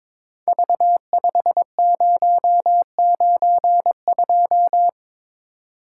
Morse code, heard as a keyed tone, about 22 wpm, 700 Hz. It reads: V5092
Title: Code Groups - 5 Letter:Numbers - 22wpm